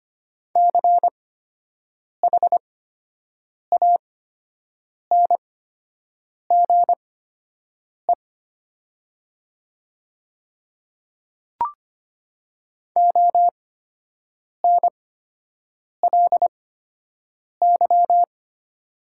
Morse code, heard as a keyed tone, about 25 wpm, spaced as for 7 wpm, 700 Hz. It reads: CHANGE ONLY
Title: 8x Character Spacing - Top 200 Words - 25wpm